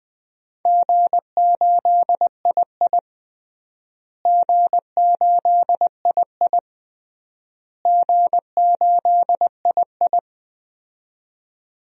Morse code, heard as a keyed tone, about 20 wpm, 700 Hz. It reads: G8II G8II G8II
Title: DXCC-100 Call Signs - 4 Characters - Repeated 3x - 20wpm